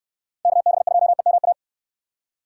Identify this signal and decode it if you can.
Morse code, heard as a keyed tone, about 69 wpm, 700 Hz. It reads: KB2EMA